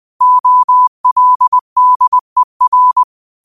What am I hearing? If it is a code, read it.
Morse code, heard as a keyed tone, about 20 wpm, 1000 Hz. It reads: OLDER
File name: qotdfiles / 20wpm